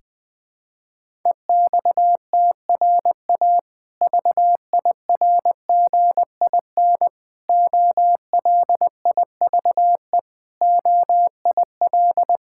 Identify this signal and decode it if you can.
Morse code, heard as a keyed tone, about 20 wpm, 700 Hz. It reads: EXTRA VIRGIN OLIVE OIL
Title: Difficult Words with Hints - 20wpm